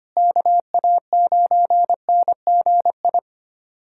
Morse code, heard as a keyed tone, about 25 wpm, 700 Hz. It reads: KA9NGI
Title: USA Callsign DE Callsign 25wpm